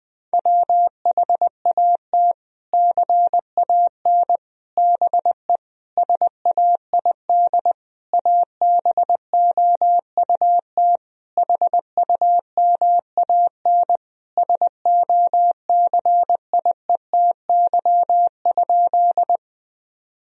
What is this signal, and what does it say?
Morse code, heard as a keyed tone, about 20 wpm, 700 Hz. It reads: WHAT CAN BE SAID ABOUT HUMAN SOCIETY?
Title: Sentences from Top 1000 Words - Review Part 4 - 20wpm